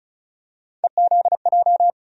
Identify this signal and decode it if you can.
Morse code, heard as a keyed tone, about 36 wpm, 700 Hz. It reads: EZJ
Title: USA Call Sign Postfixes - 3 Letters 35wpm